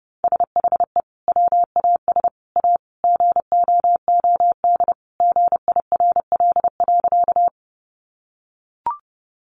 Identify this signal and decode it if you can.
Morse code, heard as a keyed tone, about 30 wpm, 700 Hz. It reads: SHE WAS A GOOD GIRL.
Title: Sentences from Top 400 Words 30wpm